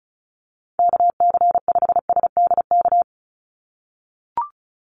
Morse code, heard as a keyed tone, about 34 wpm, 700 Hz. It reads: KC5SDK
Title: USA Call Signs 35wpm